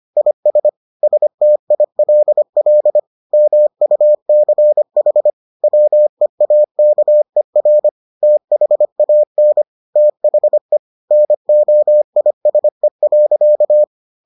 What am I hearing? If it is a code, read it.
Morse code, heard as a keyed tone, about 25 wpm, 600 Hz. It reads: IS STILL MUCH WEAKER THAN THE NOISE.